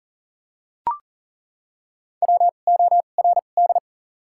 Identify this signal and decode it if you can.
Morse code, heard as a keyed tone, about 40 wpm, spaced as for 29 wpm, 700 Hz. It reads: WORD
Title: 2x Character Spacing - Top 200 Words - 40wpm